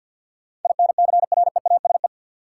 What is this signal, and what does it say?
Morse code, heard as a keyed tone, about 53 wpm, 700 Hz. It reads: INCREASE
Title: Sets of 5 Words - New Word Drill - 50wpm